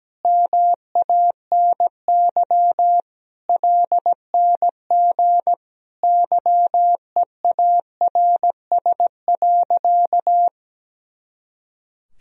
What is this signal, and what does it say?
Morse code, heard as a keyed tone, about 17 wpm, 700 Hz. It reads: MANY LNG YEARS.
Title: Sentences from Top 200 Words and Abbreviations 17wpm